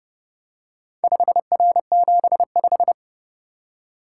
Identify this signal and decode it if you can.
Morse code, heard as a keyed tone, about 29 wpm, 700 Hz. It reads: 5R75